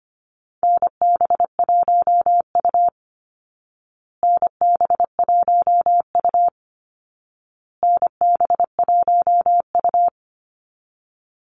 Morse code, heard as a keyed tone, about 25 wpm, 700 Hz. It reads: NB1U NB1U NB1U
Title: Call Signs Repeated 2x - 25wpm